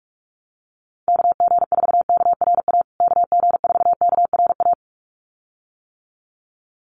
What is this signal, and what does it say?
Morse code, heard as a keyed tone, about 44 wpm, 700 Hz. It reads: KG4KRA KG4KRA
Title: USA Call Signs Repeated 2x 45wpm